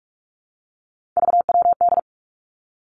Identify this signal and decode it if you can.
Morse code, heard as a keyed tone, about 46 wpm, 700 Hz. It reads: VWD
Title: Three Challenging Letter-Number Combinations 45wpm